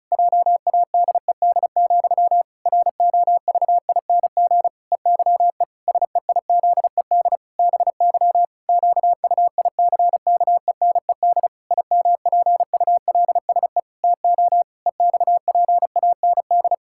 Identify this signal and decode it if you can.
Morse code, heard as a keyed tone, about 34 wpm, 700 Hz. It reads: JADED, ROVING EYE SEIZED BY QUICKENED IMPULSE TO EXPAND
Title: Pangram Challenge v2 - 35wpm